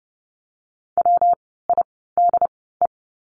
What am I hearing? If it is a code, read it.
Morse code, heard as a keyed tone, about 30 wpm, 700 Hz. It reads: W I D E
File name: course / 30:10wpm / 115 Lesson 19f - Letter w - Words 30wpm